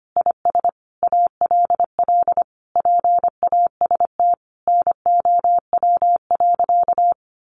Morse code, heard as a keyed tone, about 25 wpm, 700 Hz. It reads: IS ALL PAST NOW.
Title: Sentences from Top 600 Words 25wpm